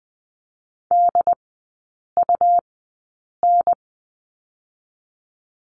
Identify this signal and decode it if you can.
Morse code, heard as a keyed tone, about 20 wpm, spaced as for 8 wpm, 700 Hz. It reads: DUN
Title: 100 Lesson 17e - Letter u - Three Characters 20wpm